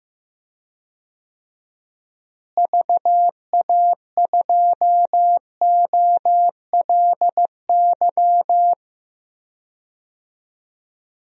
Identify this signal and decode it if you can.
Morse code, heard as a keyed tone, about 15 wpm, 700 Hz. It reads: VA2OLY
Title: DXCC-100 Call Signs - 6 Characters - Repeated 2x - 15wpm